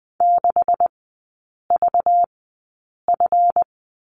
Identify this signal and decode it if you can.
Morse code, heard as a keyed tone, about 20 wpm, spaced as for 8 wpm, 700 Hz. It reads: 6VF
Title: 191 Lesson 31e - Letter v - Three Characters 20wpm